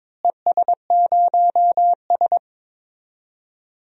Morse code, heard as a keyed tone, about 22 wpm, 700 Hz. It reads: ES0S
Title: DXCC-100 Call Signs - 4 Characters - Repeated 3x - 22wpm